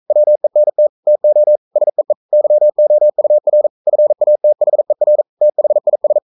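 Morse code, heard as a keyed tone, about 42 wpm, 600 Hz. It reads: WENT TO SEE YOUR FATHER THIS
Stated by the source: Morse code sending of an English novel at 42 wpm.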